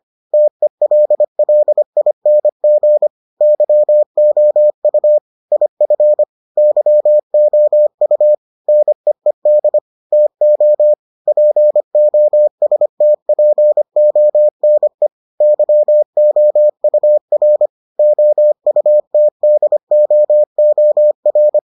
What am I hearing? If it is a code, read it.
Morse code, heard as a keyed tone, about 25 wpm, 600 Hz. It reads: TELLING YOU IF YOU NEED TO POSTPONE YOUR OUTDOOR